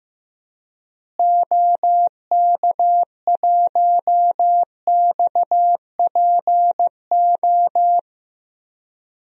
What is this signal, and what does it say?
Morse code, heard as a keyed tone, about 15 wpm, 700 Hz. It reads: OK1XPO
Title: DXCC-100 Call Signs - 6 Characters - Repeated 2x - 15wpm